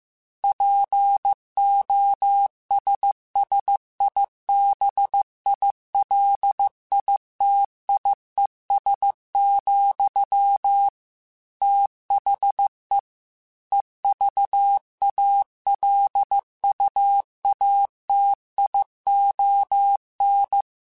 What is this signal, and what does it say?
Morse code, heard as a keyed tone, about 15 wpm, 800 Hz. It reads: POSSIBILITIES, THE EVALUATION
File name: Technical Subjects / Technical Subjects-mp3s / AND Gate @13